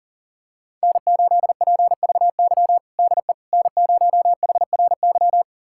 Morse code, heard as a keyed tone, about 40 wpm, 700 Hz. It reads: N8PVY DE N0HRY
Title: USA Callsign DE Callsign - 5 Characters - 40wpm